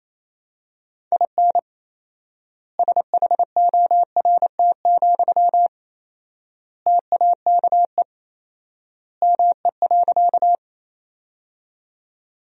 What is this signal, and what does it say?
Morse code, heard as a keyed tone, about 27 wpm, 700 Hz. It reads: IN SHORT, TAKE ME.
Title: 4x Word-Spacing - Sentences from Top 400 Words - 28wpm